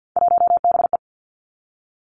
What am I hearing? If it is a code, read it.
Morse code, heard as a keyed tone, about 50 wpm, 700 Hz. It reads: 1BE
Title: Three Challenging Letter-Number Combinations 50wpm